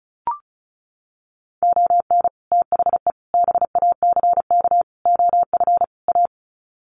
Morse code, heard as a keyed tone, about 34 wpm, 700 Hz. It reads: ON THE BACK OF A
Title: Sets of 5 Words Encore2 35wpm